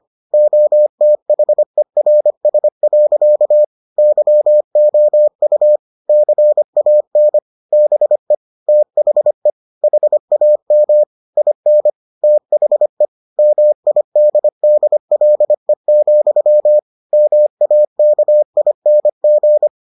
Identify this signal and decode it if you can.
Morse code, heard as a keyed tone, about 25 wpm, 600 Hz. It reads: OTHERS. YOU CAN BE THE HAM IN THE MIDDLE, MAKING